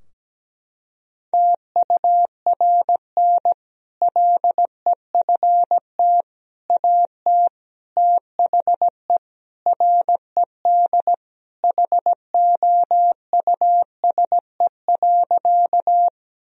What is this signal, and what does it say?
Morse code, heard as a keyed tone, about 17 wpm, 700 Hz. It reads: TURN LEFT AT THE RED HOUSE.
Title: Sentences from Top 1000 Words - Review Part 2 - 17wpm